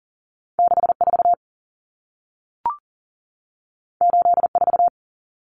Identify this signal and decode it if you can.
Morse code, heard as a keyed tone, about 40 wpm, 700 Hz. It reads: 64 84
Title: Numbers - 2-Digit 40wpm